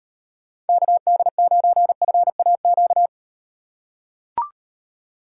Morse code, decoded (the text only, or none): KD9FAQ